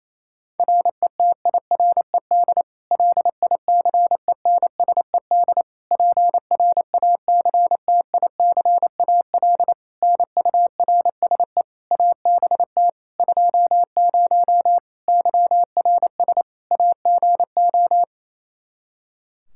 Morse code, heard as a keyed tone, about 28 wpm, 700 Hz. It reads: RETIRED LICENSED PRACTICAL NURSE ABT 20 YRS AGO
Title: Practice QSO - Exchange 3 - More Info - 28wpm